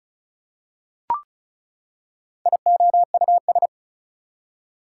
Morse code, heard as a keyed tone, about 36 wpm, 700 Hz. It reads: IOUS
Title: Prefixes and Suffixes 35wpm